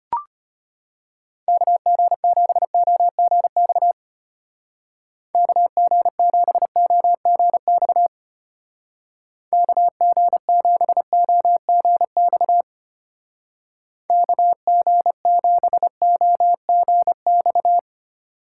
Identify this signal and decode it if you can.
Morse code, heard as a keyed tone, about 31 wpm, 700 Hz. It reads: KG7OGX KG7OGX KG7OGX KG7OGX
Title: Speed Racing - USA 2x3 Call Signs 25wpm